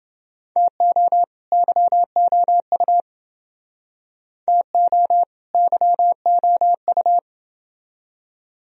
Morse code, heard as a keyed tone, about 29 wpm, 700 Hz. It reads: TO YOU TO YOU